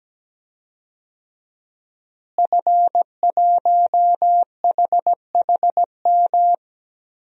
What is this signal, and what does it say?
Morse code, heard as a keyed tone, about 17 wpm, 700 Hz. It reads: F1HHM
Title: DXCC-100 Call Signs - 5 Characters - Repeated 2x - 17wpm